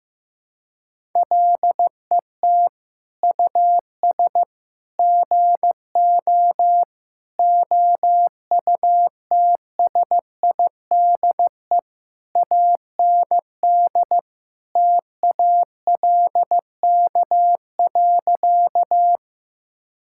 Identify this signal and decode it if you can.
Morse code, heard as a keyed tone, about 15 wpm, 700 Hz. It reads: LET US GO OUTSIDE AND TALK.
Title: Sentences from Top 1000 Words - Review Part 6 - 15wpm